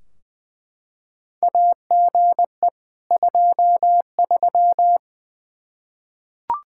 Morse code, heard as a keyed tone, about 20 wpm, 700 Hz. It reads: AGE 23
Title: Practice QSO - Exchange 3 - More Info - 20wpm